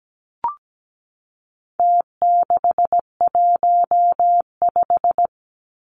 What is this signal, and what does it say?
Morse code, heard as a keyed tone, about 17 wpm, 700 Hz. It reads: T615
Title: Numbers - Contest Style - 4-Digit 17wpm